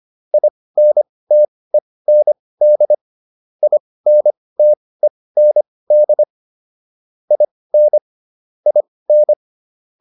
Morse code, heard as a keyed tone, about 25 wpm, spaced as for 18 wpm, 600 Hz. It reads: INTEND INTEND IN IN